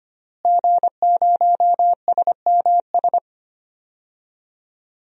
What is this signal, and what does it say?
Morse code, heard as a keyed tone, about 25 wpm, 700 Hz. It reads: G0SMS
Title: DXCC-100 Call Signs - 5 Characters - Repeated 2x - 25wpm